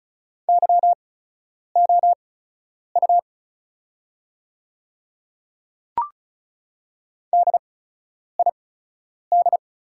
Morse code, heard as a keyed tone, about 36 wpm, spaced as for 10 wpm, 690 Hz. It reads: YOU DID
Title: 8x Character Spacing - Top 100 Words - 35wpm